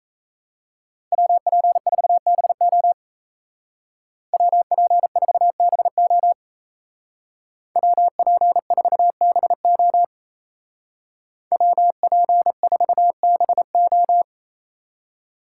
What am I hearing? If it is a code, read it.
Morse code, heard as a keyed tone, about 34 wpm, 700 Hz. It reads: WP4BO WP4BO WP4BO WP4BO
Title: Speed Racing - USA Call Signs 28wpm